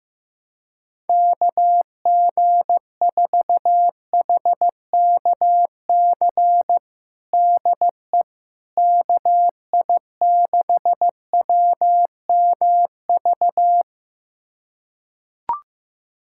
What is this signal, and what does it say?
Morse code, heard as a keyed tone, about 15 wpm, 700 Hz. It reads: KG4HKC DE KI6WMV
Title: USA Callsign DE Callsign 15wpm